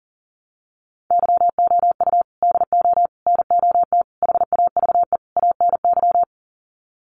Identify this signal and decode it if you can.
Morse code, heard as a keyed tone, about 40 wpm, 700 Hz. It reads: YOU DO NOT HAVE ANY
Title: Sets of 5 Words Encore 40wpm